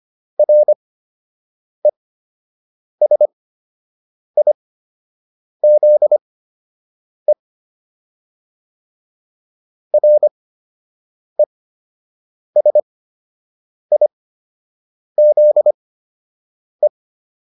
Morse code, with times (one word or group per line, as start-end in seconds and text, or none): 0.4-7.3 RESIZE
9.9-16.9 RESIZE